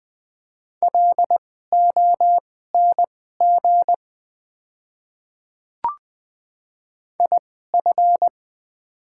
Morse code, heard as a keyed tone, about 20 wpm, spaced as for 14 wpm, 700 Hz. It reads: LONG IF
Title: 2x Character Spacing - Top 100 Words - 20wpm